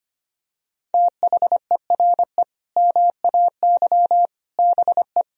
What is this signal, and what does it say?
Morse code, heard as a keyed tone, about 25 wpm, 700 Hz. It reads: THERE MAY BE
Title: Sentences from Top 1000 Words - Review Part 3 - 25wpm